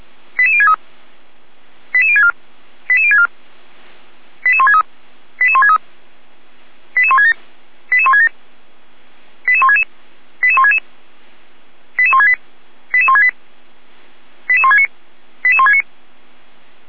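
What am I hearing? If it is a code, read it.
Five-tone selective call (ZVEI), heard as quick runs of five tones, tones of about 80 ms. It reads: 88063 88064 88064 88163 88163 88167 88167 88166 88166 88168 88168 88169 88169
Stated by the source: ZVEI five-tone call 88365 received on an SDR.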